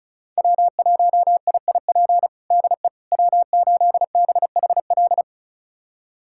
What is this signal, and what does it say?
Morse code, heard as a keyed tone, about 36 wpm, 700 Hz. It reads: W1IIP DE W8BHL
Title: USA Callsign DE Callsign - 5 Characters - 35wpm